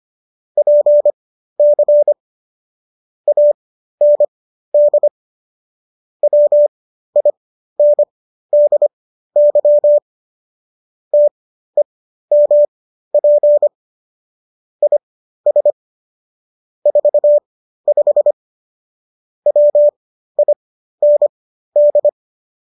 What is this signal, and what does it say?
Morse code, heard as a keyed tone, about 25 wpm, spaced as for 13 wpm, 590 Hz. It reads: PC AND WINDY TEMP IS 45 WIND